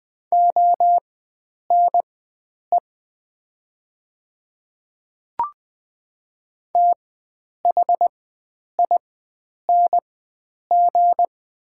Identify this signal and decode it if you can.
Morse code, heard as a keyed tone, about 20 wpm, spaced as for 9 wpm, 700 Hz. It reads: ONE THING
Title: 4x Character Spacing - Top 100 Words - 20wpm